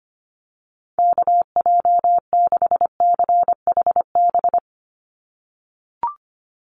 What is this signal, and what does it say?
Morse code, heard as a keyed tone, about 25 wpm, 700 Hz. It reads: KJ6CHB
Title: USA Call Signs 25wpm